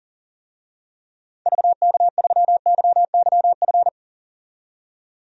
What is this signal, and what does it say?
Morse code, heard as a keyed tone, about 37 wpm, 700 Hz. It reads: VK3YYF
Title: DXCC-100 Call Signs - 6 Characters - 40wpm